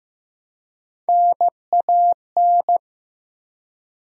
Morse code, heard as a keyed tone, about 15 wpm, 700 Hz. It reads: NAN
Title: USA Common Names - Max 3 Letters - 15wpm